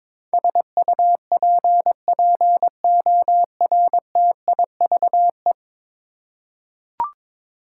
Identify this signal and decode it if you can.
Morse code, heard as a keyed tone, about 22 wpm, 700 Hz. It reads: SUPPORTIVE
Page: morse-code-ninja-practice-22wpm